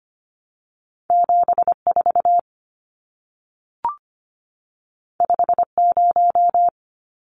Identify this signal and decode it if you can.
Morse code, heard as a keyed tone, about 25 wpm, 700 Hz. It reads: 74 50
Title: Numbers - 2-Digit 25wpm